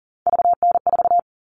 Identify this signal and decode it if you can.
Morse code, heard as a keyed tone, about 40 wpm, 700 Hz. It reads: VN4
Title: All Cut Numbers - 3-Digit 40wpm